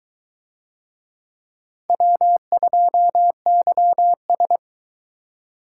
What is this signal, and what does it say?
Morse code, heard as a keyed tone, about 22 wpm, 700 Hz. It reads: W2YS